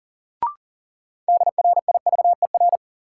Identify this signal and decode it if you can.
Morse code, heard as a keyed tone, about 40 wpm, 700 Hz. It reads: DRIVER
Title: Sentences from Top 1000 Words - New Word Drill - 40wpm